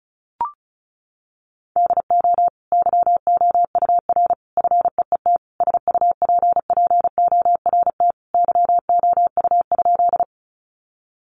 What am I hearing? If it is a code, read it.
Morse code, heard as a keyed tone, about 34 wpm, 700 Hz. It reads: DO YOUR FEET SUPPORT YOU?
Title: Sentences from Top 800 Words 35wpm